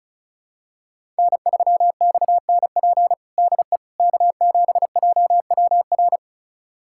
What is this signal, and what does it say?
Morse code, heard as a keyed tone, about 36 wpm, 700 Hz. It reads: N3XNP DE K7JWR
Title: USA Callsign DE Callsign - 5 Characters - 35wpm